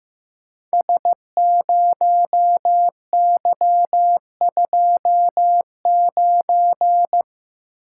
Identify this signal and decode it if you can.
Morse code, heard as a keyed tone, about 15 wpm, 700 Hz. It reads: S0Y29
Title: Code Groups - 5 Letter:Numbers - 15wpm